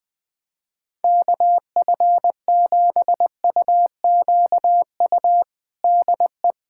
Morse code, heard as a keyed tone, about 20 wpm, 700 Hz. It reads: KF7UQU DE
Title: USA Callsign DE Callsign - 6 Characters - 20wpm